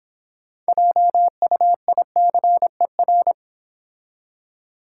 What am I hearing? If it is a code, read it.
Morse code, heard as a keyed tone, about 26 wpm, 700 Hz. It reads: JUICER